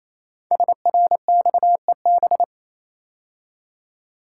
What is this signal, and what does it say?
Morse code, heard as a keyed tone, about 27 wpm, 700 Hz. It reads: SRXEB
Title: Code Groups - 5 Letters - 28wpm